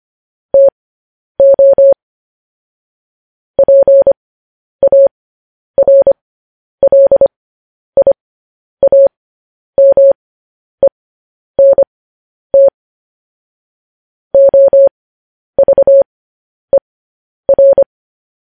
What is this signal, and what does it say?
Morse code, heard as a keyed tone, about 25 wpm, spaced as for 10 wpm, 550 Hz. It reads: TO PARLIAMENT OVER